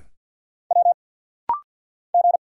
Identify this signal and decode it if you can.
Morse code, heard as a keyed tone, about 50 wpm, 700 Hz. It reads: W G
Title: Single Letters - Rapid-Fire 50wpm